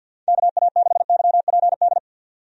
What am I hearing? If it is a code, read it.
Morse code, heard as a keyed tone, about 53 wpm, 700 Hz. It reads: KA6YPD
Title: USA Call Signs - 6 Characters - 50wpm